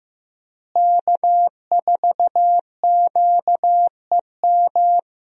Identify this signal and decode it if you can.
Morse code, heard as a keyed tone, about 16 wpm, 700 Hz. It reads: K4QEM
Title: USA Call Signs - 5 Characters - Repeated 2x - 15wpm